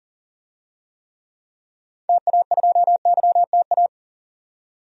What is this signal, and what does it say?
Morse code, heard as a keyed tone, about 40 wpm, 700 Hz. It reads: TA2YTA
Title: DXCC-100 Call Signs - 6 Characters - 40wpm